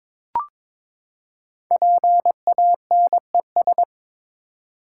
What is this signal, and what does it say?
Morse code, heard as a keyed tone, about 22 wpm, 700 Hz. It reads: PANES
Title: Code Groups - 5 Letters - 22wpm